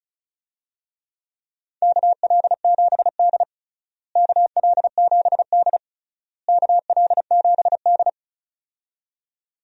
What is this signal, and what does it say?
Morse code, heard as a keyed tone, about 36 wpm, 700 Hz. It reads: KL7D KL7D KL7D
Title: DXCC-100 Call Signs - 4 Characters - Repeated 3x - 35wpm